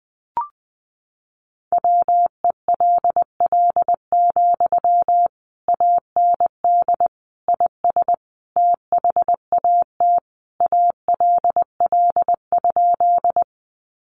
Morse code, heard as a keyed tone, about 20 wpm, 700 Hz. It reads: WELL, AND IS THAT ALL?
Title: Sentences from Top 200 Words 20wpm